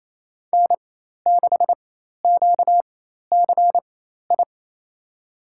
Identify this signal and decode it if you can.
Morse code, heard as a keyed tone, about 28 wpm, spaced as for 13 wpm, 700 Hz. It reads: N6QCI
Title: USA Call Signs - 4x Character Spacing - 28wpm